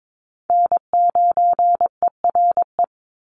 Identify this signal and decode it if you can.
Morse code, heard as a keyed tone, about 22 wpm, 700 Hz. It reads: N9ERE
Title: USA Call Signs 22wpm